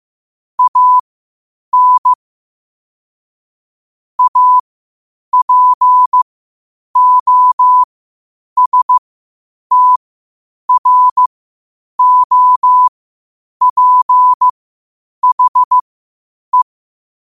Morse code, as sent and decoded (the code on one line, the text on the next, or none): .- -. / .- .--. --- ... - .-. --- .--. .... .
AN APOSTROPHE